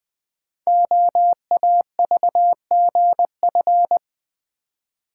Morse code, heard as a keyed tone, about 20 wpm, 700 Hz. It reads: OAVGF
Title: Code Groups - 5 Letters - 20wpm